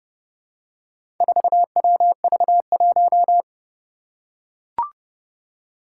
Morse code, heard as a keyed tone, about 30 wpm, 700 Hz. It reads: 4WV1
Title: Four Challenging Letter-Number Combinations 30wpm